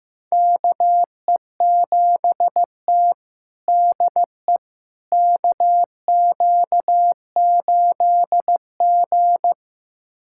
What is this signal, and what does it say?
Morse code, heard as a keyed tone, about 16 wpm, 700 Hz. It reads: KE7T DE KQ8G
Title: USA Callsign DE Callsign - 4 Characters - 15wpm